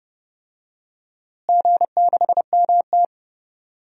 Morse code, heard as a keyed tone, about 30 wpm, 700 Hz. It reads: G6MT
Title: DXCC-100 Call Signs - 4 Characters - Repeated 3x - 30wpm